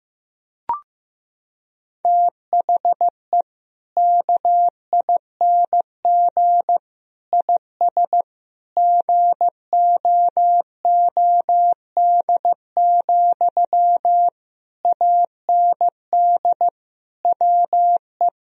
Morse code, heard as a keyed tone, about 15 wpm, 700 Hz. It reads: THE KING IS GOOD, AND WE
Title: Sentences from Top 1000 Words - Review Part 1 - 15wpm